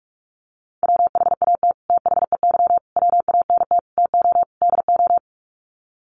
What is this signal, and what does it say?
Morse code, heard as a keyed tone, about 44 wpm, 700 Hz. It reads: WHAT THEY WANT TO DO
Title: Sets of 5 Words Encore2 45wpm